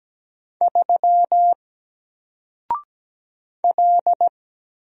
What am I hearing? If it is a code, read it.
Morse code, heard as a keyed tone, about 17 wpm, 700 Hz. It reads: 3 L
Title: Single Letter-Number 17wpm